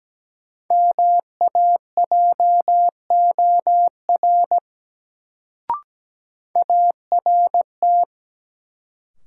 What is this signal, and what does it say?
Morse code, heard as a keyed tone, about 17 wpm, 700 Hz. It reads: MAJOR ART
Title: Sentences from Top 600 Words - New Word Drill - 17wpm